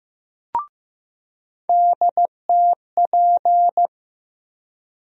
Morse code, heard as a keyed tone, about 15 wpm, 700 Hz. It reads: DTP
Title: USA Call Sign Postfixes - 3 Letters 15wpm